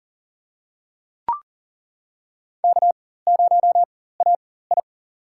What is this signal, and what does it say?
Morse code, heard as a keyed tone, about 40 wpm, spaced as for 19 wpm, 700 Hz. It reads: K0AI
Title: USA Call Signs - 4 Characters - 4x Character Spacing - 40wpm